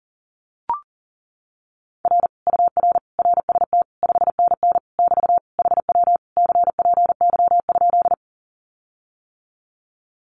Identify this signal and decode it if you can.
Morse code, heard as a keyed tone, about 37 wpm, 700 Hz. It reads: R UR RST 5NN <BT> HW CPY?
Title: Common QSO Phrases Encore 40wpm